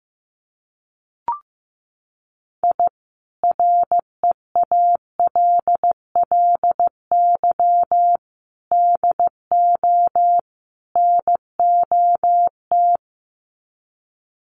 Morse code, heard as a keyed tone, about 15 wpm, 700 Hz. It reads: I REALLY DO NOT
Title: Sets of 4 Words 15wpm